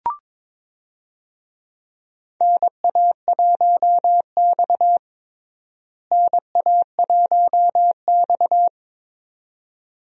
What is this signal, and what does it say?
Morse code, heard as a keyed tone, about 21 wpm, 700 Hz. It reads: NA1X NA1X
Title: USA Call Signs - 4 Characters - Repeated 2x - 22wpm